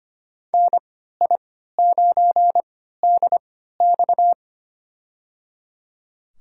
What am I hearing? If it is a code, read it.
Morse code, heard as a keyed tone, about 25 wpm, spaced as for 14 wpm, 700 Hz. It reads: NI9DX
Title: USA Call Signs - 3x Character Spacing - 25wpm